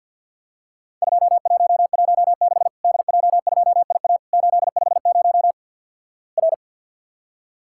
Morse code, heard as a keyed tone, about 49 wpm, 700 Hz. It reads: 2116 DJ2IA 850 R
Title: Worked All Europe DX Contest - QTC Exchange - 50wpm